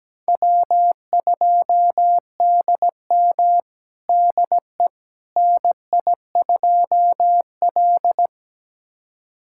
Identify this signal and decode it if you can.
Morse code, heard as a keyed tone, about 17 wpm, 700 Hz. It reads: W2DM DE NI2L
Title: USA Callsign DE Callsign - 4 Characters - 17wpm